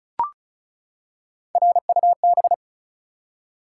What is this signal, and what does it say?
Morse code, heard as a keyed tone, about 36 wpm, 700 Hz. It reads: RUB
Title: Three Letter Words - 35wpm